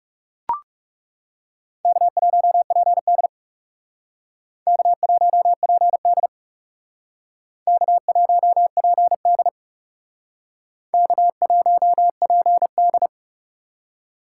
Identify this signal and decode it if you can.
Morse code, heard as a keyed tone, about 38 wpm, 700 Hz. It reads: K1PD K1PD K1PD K1PD